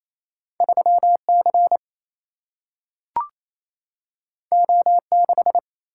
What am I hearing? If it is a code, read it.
Morse code, heard as a keyed tone, about 28 wpm, 700 Hz. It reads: 3 C O 6
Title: Single-Letter Single-Number 28wpm